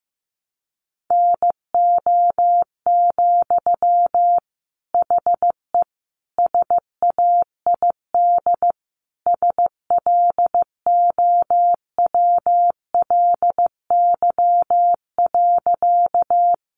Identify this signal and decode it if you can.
Morse code, heard as a keyed tone, about 15 wpm, 700 Hz. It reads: NO, HE SAID SLOWLY.
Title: Sentences from Top 1100 Words 15wpm